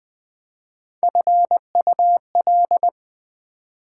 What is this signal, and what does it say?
Morse code, heard as a keyed tone, about 20 wpm, 700 Hz. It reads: FUL